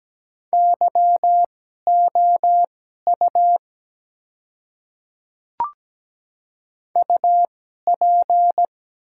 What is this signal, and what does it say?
Morse code, heard as a keyed tone, about 17 wpm, spaced as for 12 wpm, 700 Hz. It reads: YOU UP
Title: 2x Character Spacing - Top 300 Words - 17wpm